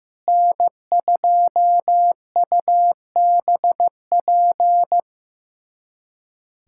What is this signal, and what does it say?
Morse code, heard as a keyed tone, about 15 wpm, 700 Hz. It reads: N2UBP
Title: USA Callsign DE Callsign - 5 Characters - 15wpm